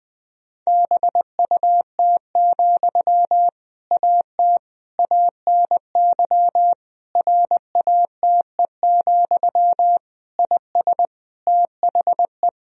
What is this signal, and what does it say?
Morse code, heard as a keyed tone, about 20 wpm, 700 Hz. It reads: BUT, AT ANY RATE, IS THE
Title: Sentences from Top 1000 Words - Review Part 4 - 20wpm